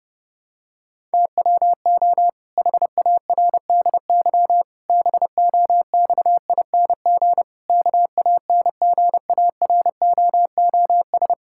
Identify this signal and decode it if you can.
Morse code, heard as a keyed tone, about 30 wpm, 700 Hz. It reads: TWO HARDY BOXING KANGAROOS
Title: Pangram Challenge v2 - 30wpm